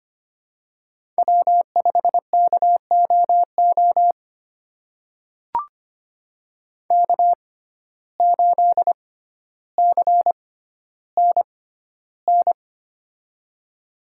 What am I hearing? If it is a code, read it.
Morse code, heard as a keyed tone, about 25 wpm, spaced as for 13 wpm, 700 Hz. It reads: W5KOO K8CNN
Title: USA Call Signs - 6x Character Spacing - 25wpm